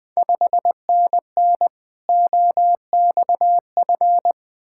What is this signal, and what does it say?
Morse code, heard as a keyed tone, about 20 wpm, 700 Hz. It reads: 5NN OXF
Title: Ontario QSO Party Exchange 20wpm